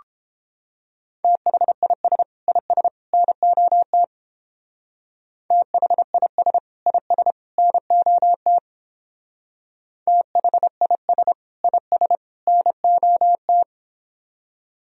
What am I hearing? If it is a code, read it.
Morse code, heard as a keyed tone, about 29 wpm, 700 Hz. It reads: THIS IS NOT THIS IS NOT THIS IS NOT